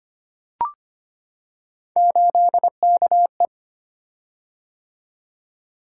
Morse code, heard as a keyed tone, about 25 wpm, 700 Hz. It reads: 8KE
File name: Three Letter-Number Combinations - 25wpm